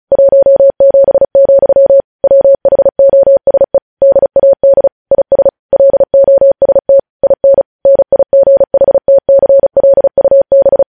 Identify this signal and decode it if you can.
Morse code, heard as a keyed tone, about 36 wpm, 550 Hz. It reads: 17, WHOSE DAD IS LOST IN NIGHTCLUB
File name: podcast / News headlines-1x-35x35